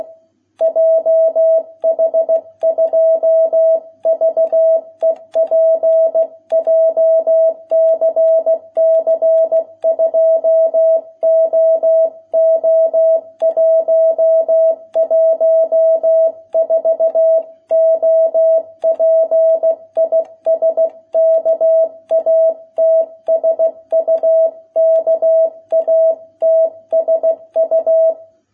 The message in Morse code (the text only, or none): JH2VEP JCC 2OO114 OP IS KATSU KATSU